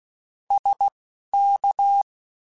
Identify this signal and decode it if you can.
Morse code, heard as a keyed tone, about 16 wpm, spaced as for 12 wpm, 800 Hz. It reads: SK
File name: QSOs Spaced